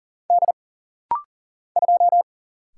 Morse code, heard as a keyed tone, about 39 wpm, 700 Hz. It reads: D 2
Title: Single Letter-Number Alt - Rapid-Fire 40wpm